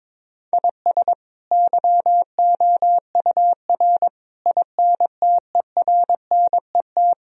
Morse code, heard as a keyed tone, about 22 wpm, 700 Hz. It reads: IS YOUR INTERNET
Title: Sentences from Top 1000 Words - Review Part 4 - 22wpm